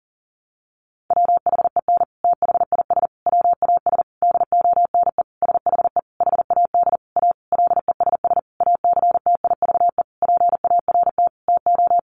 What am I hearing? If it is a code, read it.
Morse code, heard as a keyed tone, about 40 wpm, 700 Hz. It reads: WHEN THIS WAS DONE SHE HAD A LESS ACTIVE PART TO
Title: Sentences from Top 900 Words 40wpm